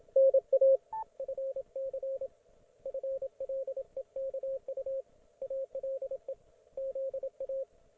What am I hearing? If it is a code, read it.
Morse code, heard as a keyed tone, about 27 wpm, 540 Hz, with radio noise. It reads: NA FC FLEKU ALE ZA